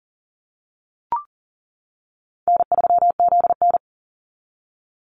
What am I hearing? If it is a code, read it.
Morse code, heard as a keyed tone, about 40 wpm, 700 Hz. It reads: N3ZN
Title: Call Signs - 40wpm